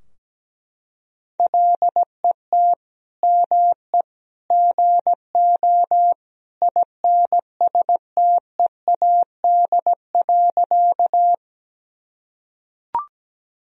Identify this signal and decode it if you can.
Morse code, heard as a keyed tone, about 17 wpm, 700 Hz. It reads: LET ME GO INSTEAD.
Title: Sentences from Top 1000 Words - Review Part 4 - 17wpm